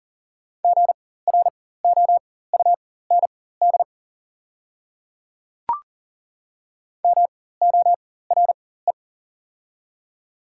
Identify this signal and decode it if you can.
Morse code, heard as a keyed tone, about 40 wpm, spaced as for 19 wpm, 690 Hz. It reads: GROUND MORE
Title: 4x Character Spacing - Top 500 Words - 40wpm